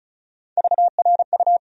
Morse code, heard as a keyed tone, about 36 wpm, 700 Hz. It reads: VRU
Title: USA Call Sign Postfixes - 3 Letters 35wpm